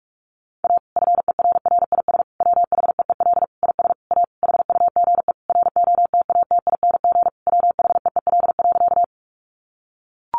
Morse code, heard as a keyed tone, about 44 wpm, 700 Hz. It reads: A FERRIS WHEEL IS A HUGE ROTATING WHEEL.
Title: Additional Easy Phrases 45wpm